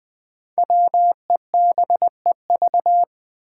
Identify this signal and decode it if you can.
Morse code, heard as a keyed tone, about 20 wpm, 700 Hz. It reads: WEBEV
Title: Five Challenging Letter-Number Combinations 20wpm